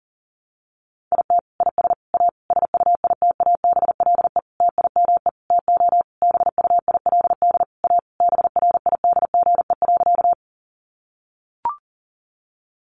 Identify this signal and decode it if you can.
Morse code, heard as a keyed tone, about 40 wpm, 700 Hz. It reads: IT IS A SUITABLE TIME TO BUILD A BRIDGE.
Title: Sentences from Top 1400 Words 40wpm